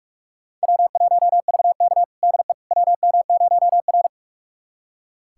Morse code, heard as a keyed tone, about 42 wpm, 700 Hz. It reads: W1VK DE WM0R
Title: USA Callsign DE Callsign - 4 Characters - 45wpm